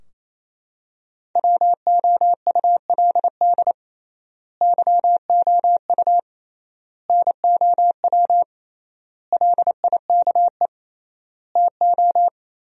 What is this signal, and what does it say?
Morse code, heard as a keyed tone, about 28 wpm, 700 Hz. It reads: WOULD YOU NOW LIKE TO
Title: 3x Word-Spacing - Sentences from Top 200 Words - 28wpm